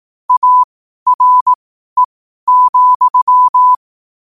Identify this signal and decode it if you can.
Morse code, heard as a keyed tone, about 18 wpm, spaced as for 12 wpm, 1000 Hz. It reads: ARE,
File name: qotdfiles / 12wpm